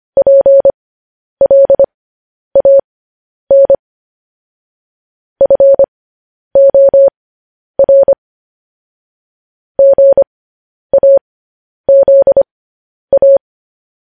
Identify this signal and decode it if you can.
Morse code, heard as a keyed tone, about 25 wpm, spaced as for 10 wpm, 550 Hz. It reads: PLAN FOR GAZA
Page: news-headlines-in-morse-code-at-10-wpm